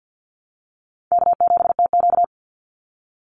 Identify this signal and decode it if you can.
Morse code, heard as a keyed tone, about 48 wpm, 700 Hz. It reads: K7TQ
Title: NA SOTA Chaser Callsigns 50wpm